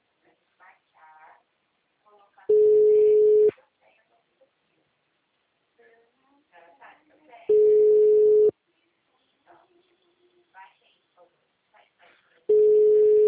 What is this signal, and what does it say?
Ringback tone (Europe, 425 Hz), with speech.